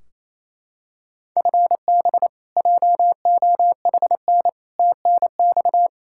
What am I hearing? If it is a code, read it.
Morse code, heard as a keyed tone, about 28 wpm, 700 Hz. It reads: FB JOHN TNX
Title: Practice QSO - Exchange 3 - More Info - 28wpm